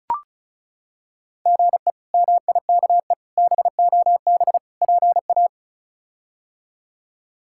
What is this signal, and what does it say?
Morse code, heard as a keyed tone, about 34 wpm, 700 Hz. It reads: GE MIKE BOB PA
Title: K1USN SST QSO Practice - 35wpm